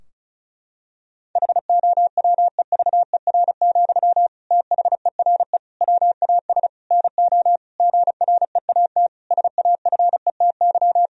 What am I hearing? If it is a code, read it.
Morse code, heard as a keyed tone, about 36 wpm, 700 Hz. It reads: HOWEVER, THERE WAS NO GREAT SAFETY
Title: Sentences from Top 1000 Words - Review Part 4 - 35wpm